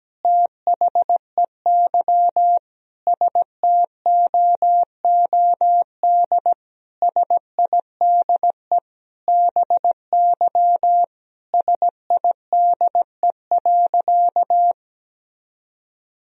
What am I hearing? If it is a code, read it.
Morse code, heard as a keyed tone, about 17 wpm, 700 Hz. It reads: THEY STOOD SIDE BY SIDE.
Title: Sentences from Top 1000 Words - Review Part 2 - 17wpm